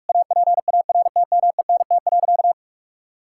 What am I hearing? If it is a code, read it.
Morse code, heard as a keyed tone, about 46 wpm, 700 Hz. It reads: APARTMENT.
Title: Sentences from Top 1000 Words - Review Part 6 - 45wpm